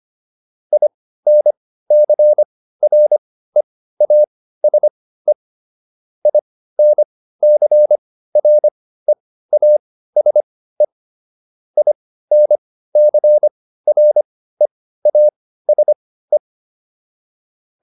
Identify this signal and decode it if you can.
Morse code, heard as a keyed tone, about 25 wpm, spaced as for 15 wpm, 600 Hz. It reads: INCREASE INCREASE INCREASE
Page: intermediate-practice-files